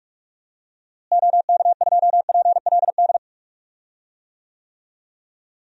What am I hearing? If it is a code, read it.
Morse code, heard as a keyed tone, about 42 wpm, 700 Hz. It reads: OK2PLD
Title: DXCC-100 Call Signs - 6 Characters - Repeated 3x - 45wpm